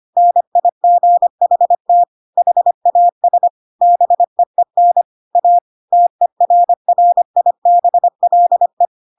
Morse code, heard as a keyed tone, about 25 wpm, 700 Hz. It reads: NIGHT HAS BEEN A TERRIBLE
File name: Pinocchio / Pinocchio ch7 25 25 ik1hge